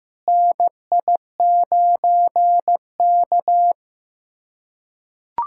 Morse code, heard as a keyed tone, about 15 wpm, 700 Hz. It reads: NI9K
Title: USA Callsign DE Callsign - 4 Characters - 15wpm